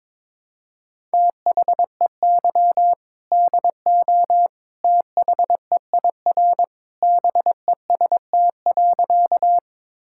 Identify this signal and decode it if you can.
Morse code, heard as a keyed tone, about 22 wpm, 700 Hz. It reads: THEY DO THEIR BEST.
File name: practice / 22wpm / Sentences with 2-Word Phrases - 22wpm